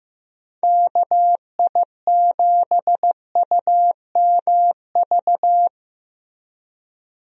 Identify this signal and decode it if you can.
Morse code, heard as a keyed tone, about 16 wpm, 700 Hz. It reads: KI7UMV
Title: USA Call Signs - 6 Characters - Repeated 3x - 15wpm